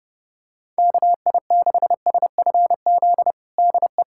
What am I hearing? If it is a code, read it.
Morse code, heard as a keyed tone, about 30 wpm, 700 Hz. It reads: KI6SFZ DE
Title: USA Callsign DE Callsign 30wpm